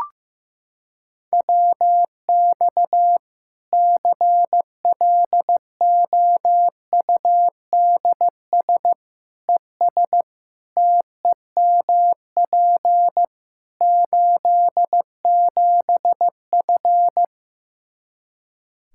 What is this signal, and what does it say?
Morse code, heard as a keyed tone, about 15 wpm, 700 Hz. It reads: WX CLOUDS ES TEMP 87F
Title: Practice QSO - Exchange 2 - Info - 15wpm